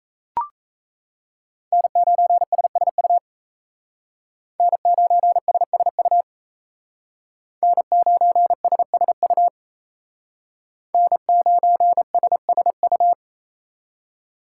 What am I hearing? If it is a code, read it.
Morse code, heard as a keyed tone, about 36 wpm, 700 Hz. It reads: N9SSU N9SSU N9SSU N9SSU